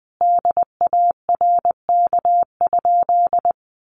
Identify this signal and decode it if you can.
Morse code, heard as a keyed tone, about 20 wpm, 700 Hz. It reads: DARK?
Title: Sentences from Top 500 Words 20wpm